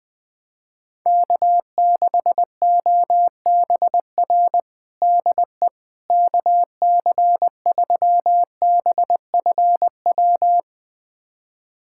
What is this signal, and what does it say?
Morse code, heard as a keyed tone, about 20 wpm, 700 Hz. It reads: K6OBR DE KC3BFW
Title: USA Callsign DE Callsign 20wpm